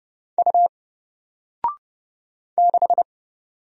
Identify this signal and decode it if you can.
Morse code, heard as a keyed tone, about 30 wpm, 700 Hz. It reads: U 6
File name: Single-Letter-Number-30wpm